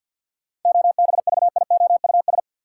Morse code, heard as a keyed tone, about 46 wpm, 700 Hz. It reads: OBVIOUS
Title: Sentences from Top 1600 Words - New Word Drill - 50wpm